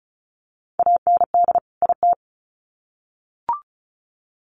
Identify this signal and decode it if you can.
Morse code, heard as a keyed tone, about 34 wpm, 700 Hz. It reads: AND IT
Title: Sets of 2 Words 35wpm